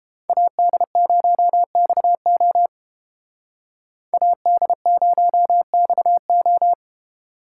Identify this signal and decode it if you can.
Morse code, heard as a keyed tone, about 31 wpm, 700 Hz. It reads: AD0XO AD0XO